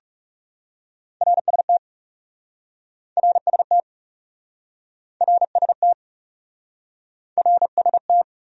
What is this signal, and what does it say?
Morse code, heard as a keyed tone, about 37 wpm, 700 Hz. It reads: RST RST RST RST
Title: Speed-Racing Abbreviations - 30wpm